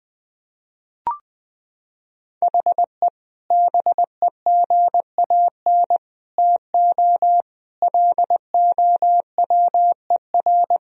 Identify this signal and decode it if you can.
Morse code, heard as a keyed tone, about 21 wpm, 700 Hz. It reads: HE BEGAN TO LOWER
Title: Sentences from Top 1000 Words - Review Part 4 - 20wpm